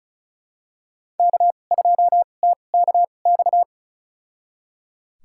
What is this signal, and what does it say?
Morse code, heard as a keyed tone, about 34 wpm, spaced as for 25 wpm, 700 Hz. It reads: K2TKX
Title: USA Call Signs - 2x Character Spacing - 35wpm